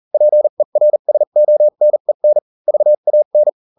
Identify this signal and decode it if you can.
Morse code, heard as a keyed tone, about 40 wpm, spaced as for 35 wpm, 600 Hz. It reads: PERSONEN VAN